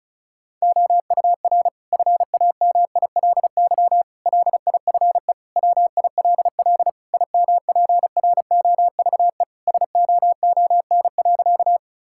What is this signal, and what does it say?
Morse code, heard as a keyed tone, about 36 wpm, 700 Hz. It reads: OUR FAMILY LIFE WILL IMPROVE SOON.
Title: Sentences from Top 1000 Words - Review Part 5 - 35wpm